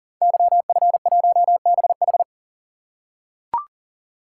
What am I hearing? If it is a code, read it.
Morse code, heard as a keyed tone, about 38 wpm, 690 Hz. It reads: YF1BH
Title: Code Groups - 5 Letter:Numbers - 40wpm